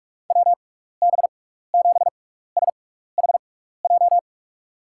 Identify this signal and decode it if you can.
Morse code, heard as a keyed tone, about 44 wpm, spaced as for 15 wpm, 700 Hz. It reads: WB7SHJ
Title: USA Call Signs - 6 Characters - 6x Character Spacing - 45wpm